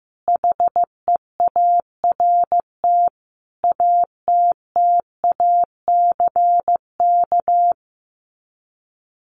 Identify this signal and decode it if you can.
Morse code, heard as a keyed tone, about 15 wpm, 700 Hz. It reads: HEART ATTACK
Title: Sets of 5 Words Encore2 15wpm